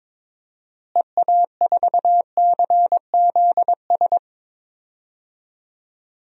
Morse code, heard as a keyed tone, about 21 wpm, 700 Hz. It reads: EA4CZS
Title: DXCC-100 Call Signs - 6 Characters - Repeated 3x - 22wpm